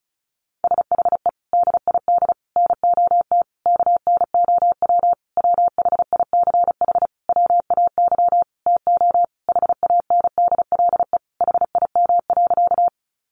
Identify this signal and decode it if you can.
Morse code, heard as a keyed tone, about 34 wpm, 700 Hz. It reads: SHE DID NOT KNOW WHICH WAY TO HANDLE HIM.
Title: Sentences from Top 1100 Words 35wpm